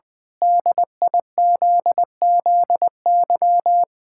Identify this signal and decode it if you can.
Morse code, heard as a keyed tone, about 20 wpm, 700 Hz. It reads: DIZZY
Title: Pangram Challenge v2 - 20wpm